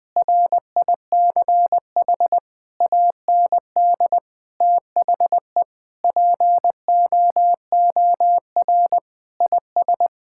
Morse code, heard as a keyed tone, about 20 wpm, 700 Hz. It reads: RICH AND THE POOR IS
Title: Sentences with 8-Word Phrases - 20wpm